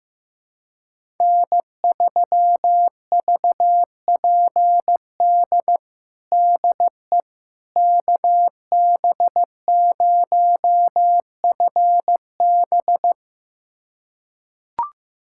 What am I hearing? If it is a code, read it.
Morse code, heard as a keyed tone, about 15 wpm, 700 Hz. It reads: N3VPD DE KB0FB
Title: USA Callsign DE Callsign - 5 Characters - 15wpm